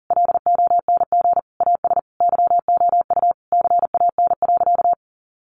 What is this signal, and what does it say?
Morse code, heard as a keyed tone, about 40 wpm, 700 Hz. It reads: LONG AS YOU CAN.
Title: Sentences from Top 1200 Words 40wpm